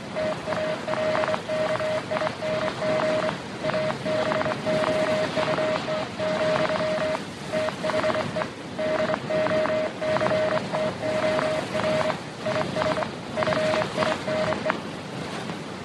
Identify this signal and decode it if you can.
Morse code, heard as a keyed tone, about 25 wpm, 620 Hz. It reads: TALKING ABOUT, THE DOCTOR IS FINE